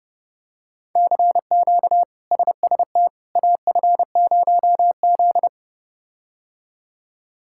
Morse code, heard as a keyed tone, about 30 wpm, 700 Hz. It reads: CQ SST AF0Z
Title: K1USN SST QSO Practice - 30wpm